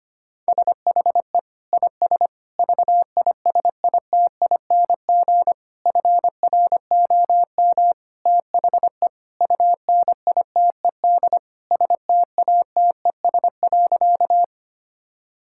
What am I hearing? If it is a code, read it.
Morse code, heard as a keyed tone, about 25 wpm, 700 Hz. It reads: SHE IS VISITING FROM THE UNITED STATES.
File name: practice / 25wpm / Sentences with 4-Word Phrases - 25wpm